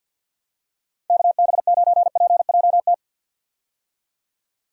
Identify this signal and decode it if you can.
Morse code, heard as a keyed tone, about 50 wpm, 700 Hz. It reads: KB9PJT
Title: USA Call Signs - 6 Characters - 50wpm